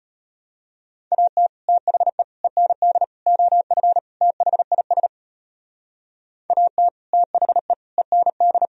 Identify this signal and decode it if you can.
Morse code, heard as a keyed tone, about 32 wpm, 700 Hz. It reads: AT THE END OF THIS AT THE END